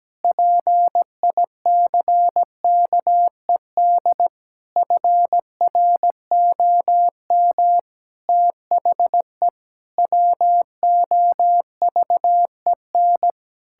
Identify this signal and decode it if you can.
Morse code, heard as a keyed tone, about 17 wpm, 700 Hz. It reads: PICKED FROM THE WOVEN
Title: Pangram Challenge v2 - 17wpm